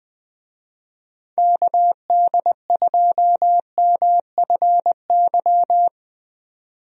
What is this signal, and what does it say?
Morse code, heard as a keyed tone, about 21 wpm, 700 Hz. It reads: KD2MFY
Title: USA Call Signs - 6 Characters - 20wpm